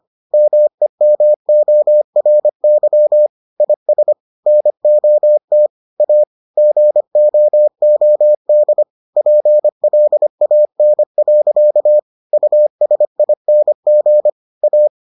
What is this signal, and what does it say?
Morse code, heard as a keyed tone, about 25 wpm, 600 Hz. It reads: MEMORY IS NOT A GOOD PLAN. USING A